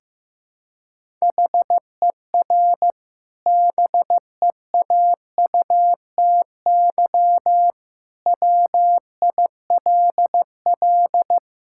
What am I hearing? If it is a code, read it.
Morse code, heard as a keyed tone, about 15 wpm, 700 Hz. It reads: HER BEAUTY WILL
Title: Sentences from Top 1000 Words - Review Part 5 - 15wpm